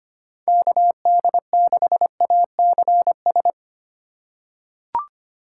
Morse code, heard as a keyed tone, about 25 wpm, 700 Hz. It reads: KD6ACS
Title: USA Call Signs - 6 Characters - 4x Character Spacing - 25wpm